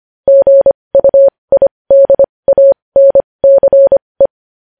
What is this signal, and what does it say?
Morse code, heard as a keyed tone, about 25 wpm, spaced as for 20 wpm, 550 Hz. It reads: GUIDANCE
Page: news headlines in morse code at 20 wpm